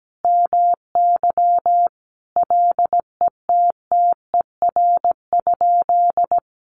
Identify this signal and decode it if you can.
Morse code, heard as a keyed tone, about 17 wpm, 700 Hz. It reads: MY LETTER?
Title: Sentences from Top 600 Words 17wpm